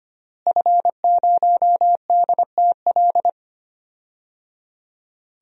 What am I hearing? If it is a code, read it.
Morse code, heard as a keyed tone, about 25 wpm, 700 Hz. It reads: F0DTL
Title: DXCC-100 Call Signs - 5 Characters - Repeated 3x - 25wpm